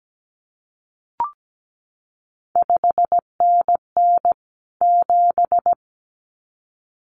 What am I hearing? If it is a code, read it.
Morse code, heard as a keyed tone, about 17 wpm, 700 Hz. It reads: 5NN 7
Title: CQ WW DX Contest Exchange Encore - 17wpm